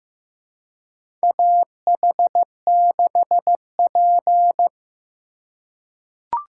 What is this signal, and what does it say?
Morse code, heard as a keyed tone, about 15 wpm, 700 Hz. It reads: AH6P
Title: USA Call Signs - 4 Characters - 4x Character Spacing - 15wpm